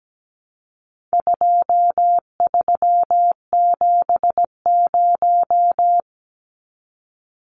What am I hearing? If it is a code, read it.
Morse code, heard as a keyed tone, about 17 wpm, 700 Hz. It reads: 2370
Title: Numbers - 4-Digit 17wpm